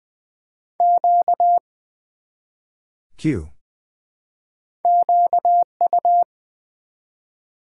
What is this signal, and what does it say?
Morse code, heard as a keyed tone, about 20 wpm, 700 Hz. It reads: Q QU
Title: Build A Word - 1 Letter At A Time - 3 to 5-Letter Words - 20wpm